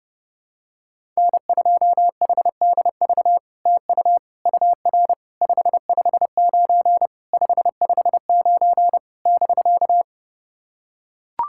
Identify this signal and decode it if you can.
Morse code, heard as a keyed tone, about 30 wpm, 700 Hz. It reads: N2HDV TU UR 559 559 <BK>
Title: SOTA QSO Practice 30wpm